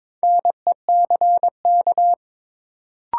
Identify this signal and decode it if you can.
Morse code, heard as a keyed tone, about 22 wpm, 700 Hz. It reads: NECK
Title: Sets of 5 Words Encore - New Word Drill - 22wpm